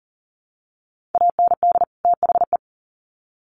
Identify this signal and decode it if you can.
Morse code, heard as a keyed tone, about 40 wpm, 700 Hz. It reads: AND THE
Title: Sets of 2 Words 40wpm